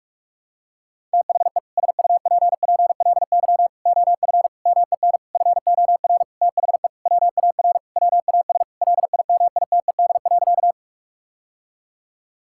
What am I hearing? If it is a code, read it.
Morse code, heard as a keyed tone, about 46 wpm, 700 Hz. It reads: THE SUPPLY OF MEN FOR THE WAR WAS LIMITED.
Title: Sentences from Top 1000 Words - Review Part 6 - 45wpm